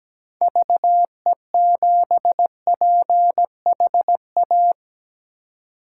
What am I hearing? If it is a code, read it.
Morse code, heard as a keyed tone, about 17 wpm, 700 Hz. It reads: VE7PHA
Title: K1USN SST QSO Practice - 17wpm